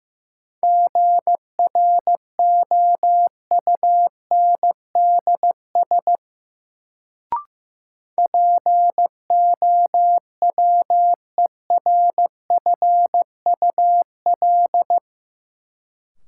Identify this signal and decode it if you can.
Morse code, heard as a keyed tone, about 15 wpm, 700 Hz. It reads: GROUNDS POWERFUL
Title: Sentences from Top 1300 Words - New Word Drill Alt - 15wpm